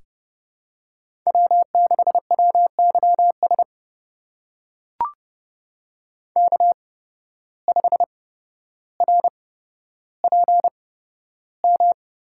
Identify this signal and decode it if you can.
Morse code, heard as a keyed tone, about 30 wpm, spaced as for 13 wpm, 700 Hz. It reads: W6WYS K5RPM
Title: USA Call Signs - 5 Characters - 8x Character Spacing - 30wpm